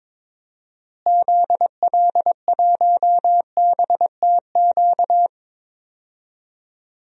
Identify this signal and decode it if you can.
Morse code, heard as a keyed tone, about 22 wpm, 700 Hz. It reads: ZL1BTQ
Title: DXCC-100 Call Signs - 6 Characters - Repeated 2x - 22wpm